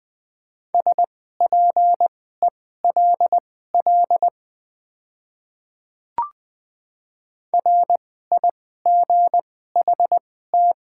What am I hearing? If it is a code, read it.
Morse code, heard as a keyed tone, about 20 wpm, spaced as for 15 wpm, 700 Hz. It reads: SPELL RIGHT
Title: 2x Character Spacing - Top 200 Words - 20wpm